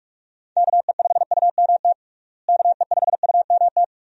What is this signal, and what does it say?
Morse code, heard as a keyed tone, about 43 wpm, 700 Hz. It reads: KE5UMT KE5UMT
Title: USA Call Signs - 6 Characters - Repeated 2x - 45wpm